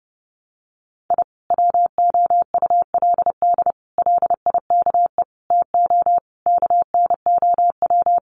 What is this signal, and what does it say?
Morse code, heard as a keyed tone, about 30 wpm, 700 Hz. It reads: I WOULD LIKE TO KNOW
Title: Sets of 5 Words Encore 30wpm